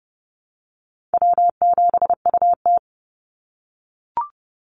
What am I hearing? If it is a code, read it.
Morse code, heard as a keyed tone, about 30 wpm, 700 Hz. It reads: W7UT
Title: Easy Call Signs - 30wpm